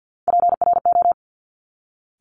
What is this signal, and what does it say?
Morse code, heard as a keyed tone, about 50 wpm, 700 Hz. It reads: PRO
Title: Top Three Letter Combinations in Words 50wpm